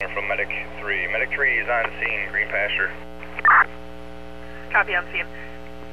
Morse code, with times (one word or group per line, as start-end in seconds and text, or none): none